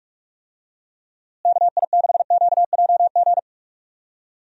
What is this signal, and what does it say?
Morse code, heard as a keyed tone, about 46 wpm, 700 Hz. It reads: KI6QJG
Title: USA Call Signs - 6 Characters - Repeated 2x - 45wpm